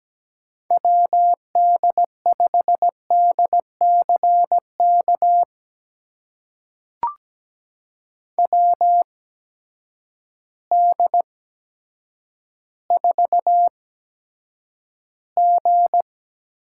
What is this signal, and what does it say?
Morse code, heard as a keyed tone, about 17 wpm, spaced as for 8 wpm, 700 Hz. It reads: WD5DCK WD4G
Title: USA Call Signs - 6 Characters - 8x Character Spacing - 17wpm